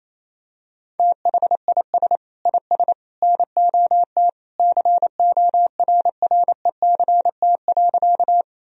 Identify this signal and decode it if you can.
Morse code, heard as a keyed tone, about 28 wpm, 700 Hz. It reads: THIS IS NOT CORRECT.
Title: Sentences from Top 1000 Words - Review Part 2 - 28wpm